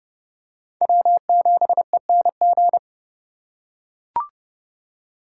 Morse code, heard as a keyed tone, about 28 wpm, 700 Hz. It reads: W7ENG